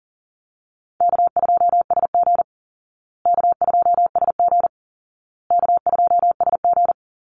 Morse code, heard as a keyed tone, about 37 wpm, 700 Hz. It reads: K2SG K2SG K2SG
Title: Easy Call Signs Repeated 3x - 40wpm